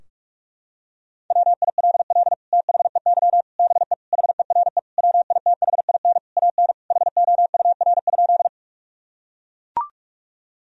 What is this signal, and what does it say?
Morse code, heard as a keyed tone, about 43 wpm, 700 Hz. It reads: WILL THEY BE HERE WITHIN AN HOUR?
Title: Sentences from Top 1000 Words - Review Part 4 - 45wpm